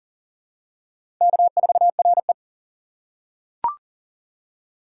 Morse code, heard as a keyed tone, about 37 wpm, 700 Hz. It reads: K4RE
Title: USA Call Signs - 2x Character Spacing - 40wpm